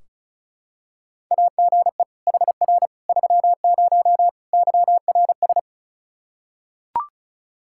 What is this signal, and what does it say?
Morse code, heard as a keyed tone, about 36 wpm, 700 Hz. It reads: AGE HR 30 YRS
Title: Practice QSO - Exchange 3 - More Info - 35wpm